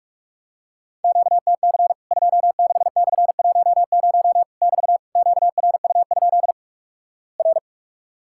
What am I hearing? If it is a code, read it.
Morse code, heard as a keyed tone, about 44 wpm, 700 Hz. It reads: QTC 26/10 <BT> QRU? R
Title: Worked All Europe DX Contest - QTC Exchange - 45wpm